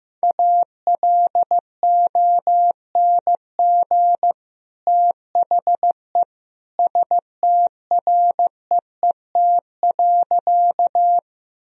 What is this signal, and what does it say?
Morse code, heard as a keyed tone, about 15 wpm, 700 Hz. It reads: ALONG THE STREET.
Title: Sentences from Top 1000 Words - Review Part 3 - 15wpm